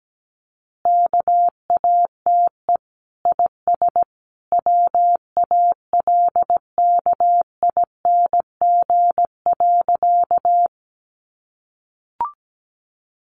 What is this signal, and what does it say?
Morse code, heard as a keyed tone, about 17 wpm, 700 Hz. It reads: KATE IS WALKING.
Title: Easy Phrases 17wpm